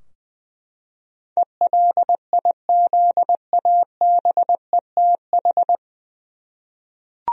Morse code, heard as a keyed tone, about 20 wpm, 700 Hz. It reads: ELIZABETH
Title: USA Common Names - 20wpm